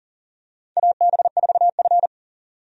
Morse code, heard as a keyed tone, about 40 wpm, 700 Hz. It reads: AB4F
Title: USA Call Signs - 4 Characters - 40wpm